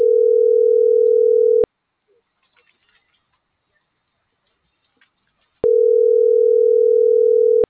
Ringback tone (North America), 440 + 480 Hz.